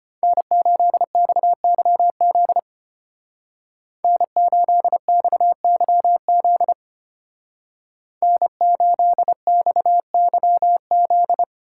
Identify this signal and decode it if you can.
Morse code, heard as a keyed tone, about 33 wpm, 700 Hz. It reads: N8XYZ N8XYZ N8XYZ